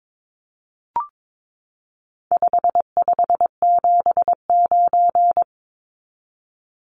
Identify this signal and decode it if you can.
Morse code, heard as a keyed tone, about 22 wpm, 700 Hz. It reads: 5579
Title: Numbers - 4-Digit 22wpm